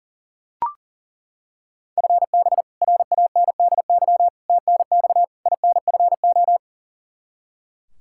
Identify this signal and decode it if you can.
Morse code, heard as a keyed tone, about 39 wpm, 700 Hz. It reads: FB RANDY TNX INFO
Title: K1USN SST QSO Practice 40wpm